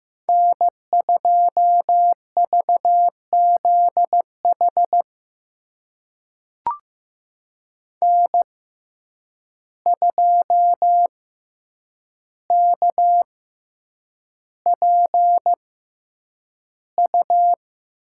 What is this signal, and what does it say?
Morse code, heard as a keyed tone, about 15 wpm, spaced as for 8 wpm, 700 Hz. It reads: N2VZH N2KPU
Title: USA Call Signs - 6x Character Spacing - 15wpm